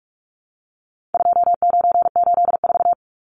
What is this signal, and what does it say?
Morse code, heard as a keyed tone, about 44 wpm, 700 Hz. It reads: 2984
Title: Numbers - 4-Digit 45wpm